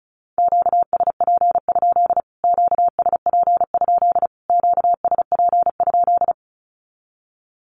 Morse code, heard as a keyed tone, about 34 wpm, 700 Hz. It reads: QSP? QSP? QSP?
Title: Q-Codes Repeated 3x 35wpm